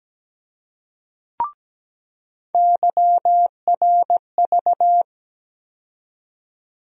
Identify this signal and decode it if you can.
Morse code, heard as a keyed tone, about 17 wpm, 700 Hz. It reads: YRV
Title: USA Call Sign Postfixes - 3 Letters 17wpm